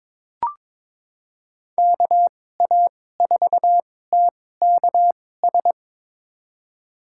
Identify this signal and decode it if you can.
Morse code, heard as a keyed tone, about 22 wpm, spaced as for 16 wpm, 700 Hz. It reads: KA4TKS
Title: USA Call Signs - 6 Characters - 2x Character Spacing - 22wpm